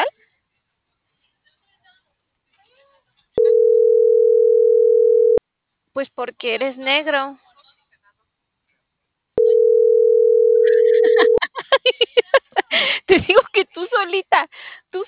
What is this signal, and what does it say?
Ringback tone (North America), 440 + 480 Hz, with speech.
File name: CENTRAL TERRESTRE / out els..laz.elo.mo.24